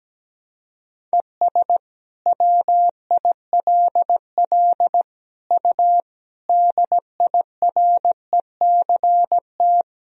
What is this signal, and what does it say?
Morse code, heard as a keyed tone, about 17 wpm, 700 Hz. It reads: ES WILL U DIRECT
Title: Sentences from Top 400 Words and Abbreviations 17wpm